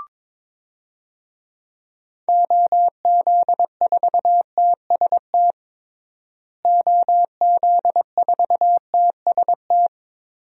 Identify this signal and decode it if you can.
Morse code, heard as a keyed tone, about 22 wpm, 700 Hz. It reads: OZ4TST OZ4TST
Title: DXCC-100 Call Signs - 6 Characters - Repeated 3x - 22wpm